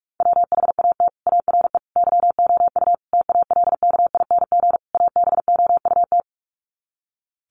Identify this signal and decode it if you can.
Morse code, heard as a keyed tone, about 44 wpm, 700 Hz. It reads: WHAT ARE YOU TALKING ABOUT
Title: Sets of 5 Words 45wpm